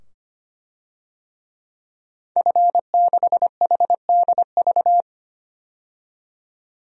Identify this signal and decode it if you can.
Morse code, heard as a keyed tone, about 25 wpm, 700 Hz. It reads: F6HDV
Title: DXCC-100 Call Signs - 5 Characters - 25wpm